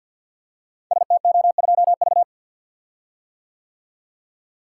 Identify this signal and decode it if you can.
Morse code, heard as a keyed tone, about 47 wpm, 700 Hz. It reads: STO2V
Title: Code Groups - 5 Letter:Numbers - 50wpm